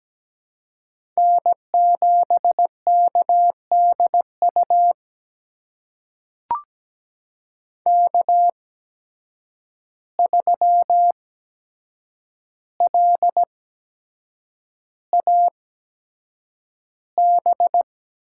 Morse code, heard as a keyed tone, about 17 wpm, spaced as for 7 wpm, 700 Hz. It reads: N7KDU K3LAB
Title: USA Call Signs - 5 Characters - 8x Character Spacing - 17wpm